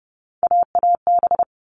Morse code, heard as a keyed tone, about 30 wpm, 700 Hz. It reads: AAB
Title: All Cut Numbers - 3-Digit 30wpm